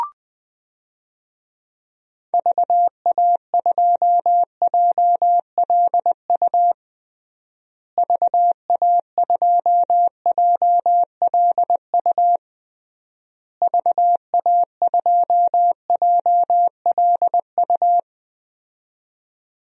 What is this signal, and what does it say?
Morse code, heard as a keyed tone, about 20 wpm, 700 Hz. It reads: VA2JLU VA2JLU VA2JLU
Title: DXCC-100 Call Signs - 6 Characters - Repeated 3x - 20wpm